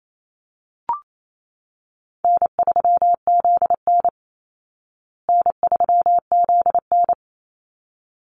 Morse code, heard as a keyed tone, about 28 wpm, 700 Hz. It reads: N3ZN N3ZN
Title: Easy Call Signs Repeated 2x - 28wpm